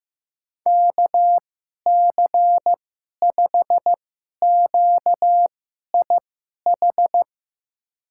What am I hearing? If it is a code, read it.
Morse code, heard as a keyed tone, about 15 wpm, spaced as for 11 wpm, 700 Hz. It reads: KC5QIH
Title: USA Call Signs - 6 Characters - 2x Character Spacing - 15wpm